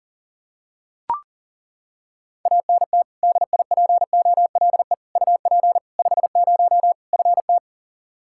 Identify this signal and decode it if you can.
Morse code, heard as a keyed tone, about 40 wpm, 700 Hz. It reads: ANT DIPOLE UP 50 FT
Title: Practice QSO - Exchange 2 - Info - 40wpm